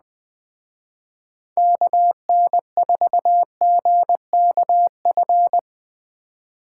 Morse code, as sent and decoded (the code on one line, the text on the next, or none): -.- -. ....- --. -.- ..-.
KN4GKF